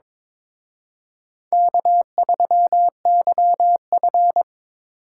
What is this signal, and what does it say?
Morse code, heard as a keyed tone, about 26 wpm, 700 Hz. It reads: K3YF
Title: Speed Racing - USA 1x2 Call Signs 22wpm